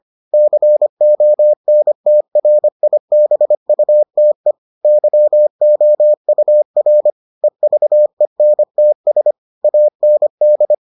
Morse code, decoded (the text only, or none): CONTRIBUTE YOUR EVENTS AND